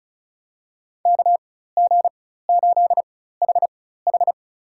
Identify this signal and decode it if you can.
Morse code, heard as a keyed tone, about 36 wpm, spaced as for 16 wpm, 700 Hz. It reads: KG8HH
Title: USA Call Signs - 5 Characters - 4x Character Spacing - 35wpm